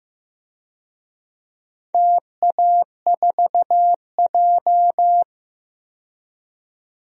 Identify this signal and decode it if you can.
Morse code, heard as a keyed tone, about 15 wpm, 700 Hz. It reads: TA4J
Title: DXCC-100 Call Signs - 4 Characters - 15wpm